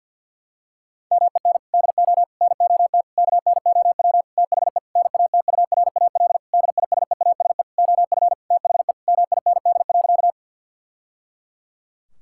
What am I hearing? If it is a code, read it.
Morse code, heard as a keyed tone, about 50 wpm, 700 Hz. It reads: MEN DO NOT KNOW THE NATURAL DISEASE OF THE MIND.
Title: Sentences from Top 1000 Words - Review Part 6 - 50wpm